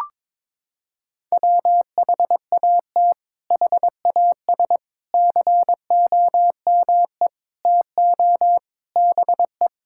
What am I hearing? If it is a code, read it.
Morse code, heard as a keyed tone, about 22 wpm, 700 Hz. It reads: WHAT HAS COME TO BE
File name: practice / 22wpm / Sentences with 8-Word Phrases - 22wpm